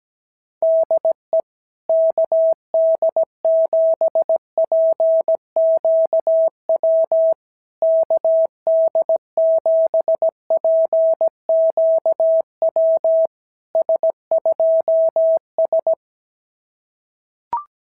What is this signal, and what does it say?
Morse code, heard as a keyed tone, about 17 wpm, 650 Hz. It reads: DE KD7PQW KD7PQW S2S
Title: SOTA QSO Practice 17wpm